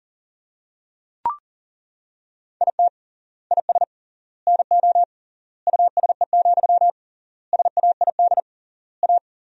Morse code, heard as a keyed tone, about 37 wpm, 700 Hz. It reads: IT IS NO USE, SAID A
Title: 3x Word-Spacing - Sentences from Top 500 Words - 40wpm